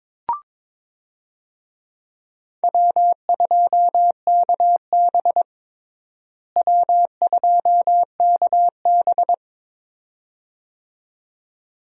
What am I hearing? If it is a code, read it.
Morse code, heard as a keyed tone, about 22 wpm, 700 Hz. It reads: W2KB W2KB
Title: USA Call Signs - 4 Characters - Repeated 2x - 22wpm